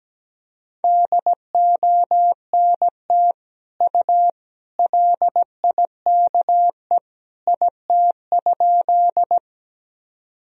Morse code, heard as a keyed tone, about 17 wpm, 700 Hz. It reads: DONT U LIKE IT?
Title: Sentences from Top 300 Words and Abbreviations 17wpm